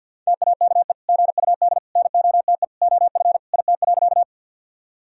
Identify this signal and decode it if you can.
Morse code, heard as a keyed tone, about 46 wpm, 700 Hz. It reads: TAKE GUD NOTE OF IT.
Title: Sentences from Top 500 Words and Abbreviations 50wpm